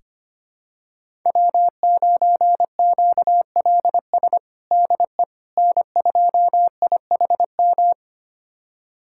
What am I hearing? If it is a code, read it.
Morse code, heard as a keyed tone, about 25 wpm, 700 Hz. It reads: W9QLS DE N2IHM
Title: USA Callsign DE Callsign - 5 Characters - 25wpm